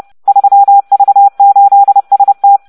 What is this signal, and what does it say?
Morse code, heard as a keyed tone, about 30 wpm, 800 Hz. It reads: 3V8ST